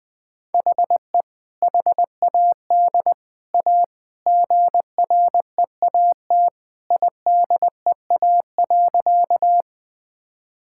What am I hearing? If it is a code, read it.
Morse code, heard as a keyed tone, about 21 wpm, 700 Hz. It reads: HE HAD A GREAT IDEA.
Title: Sentences with 3-Word Phrases - 20wpm